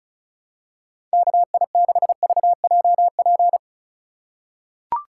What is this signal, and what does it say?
Morse code, heard as a keyed tone, about 36 wpm, 700 Hz. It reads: KI6VJP
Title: USA Call Signs - 6 Characters - 3x Character Spacing - 35wpm